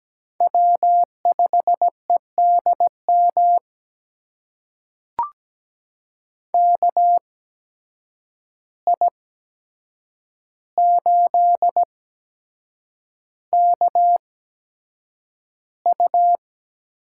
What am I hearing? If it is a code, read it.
Morse code, heard as a keyed tone, about 17 wpm, spaced as for 7 wpm, 700 Hz. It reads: W5EDM KI8KU
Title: USA Call Signs - 5 Characters - 8x Character Spacing - 17wpm